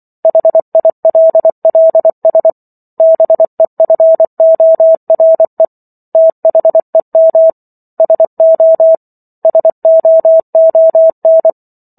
Morse code, heard as a keyed tone, about 24 wpm, 650 Hz. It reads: HILLS BEFORE THEM SO SOON